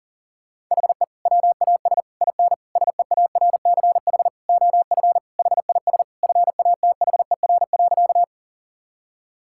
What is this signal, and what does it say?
Morse code, heard as a keyed tone, about 40 wpm, 700 Hz. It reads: HE WAS IN SEARCH OF HIS FATHER.
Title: Sentences from Top 1000 Words - Review Part 5 - 40wpm